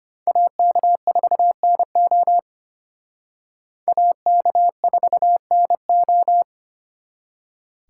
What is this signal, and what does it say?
Morse code, heard as a keyed tone, about 28 wpm, 700 Hz. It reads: AK4NO AK4NO
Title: Speed Racing - USA 2x2 Call Signs 25wpm